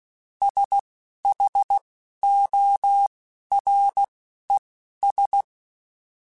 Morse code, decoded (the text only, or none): SHORES